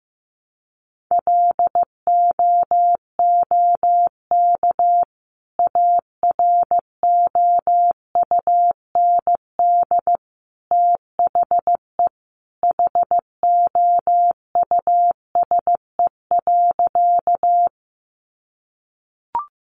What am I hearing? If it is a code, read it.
Morse code, heard as a keyed tone, about 15 wpm, 700 Hz. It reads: LOOK AROUND THE HOUSE.
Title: Sentences from Top 600 Words 15wpm